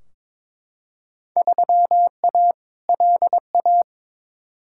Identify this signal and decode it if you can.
Morse code, heard as a keyed tone, about 22 wpm, 700 Hz. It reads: 3A LA
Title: Field Day Exchanges - 22wpm